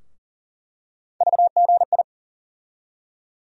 Morse code, heard as a keyed tone, about 40 wpm, 700 Hz. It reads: VGI
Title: USA Call Sign Postfixes - 3 Letters 40wpm